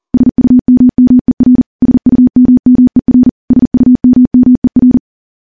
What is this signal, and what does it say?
Morse code, heard as a keyed tone, about 40 wpm, 250 Hz. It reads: SUMMER SUMMER SUMMER
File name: morse / morse-training-40WPM